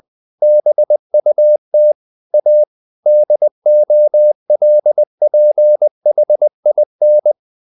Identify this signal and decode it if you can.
Morse code, heard as a keyed tone, about 20 wpm, 600 Hz. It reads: BUT A DOLPHIN
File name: the old man and the sea / the old man and the sea part 1 20wpm